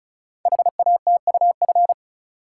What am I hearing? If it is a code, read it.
Morse code, heard as a keyed tone, about 36 wpm, 690 Hz. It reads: HATUF
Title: Code Groups - 5 Letters - 35wpm